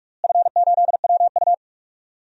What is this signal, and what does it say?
Morse code, heard as a keyed tone, about 45 wpm, 700 Hz. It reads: F8WU
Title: DXCC-100 Call Signs - 4 Characters - Repeated 3x - 45wpm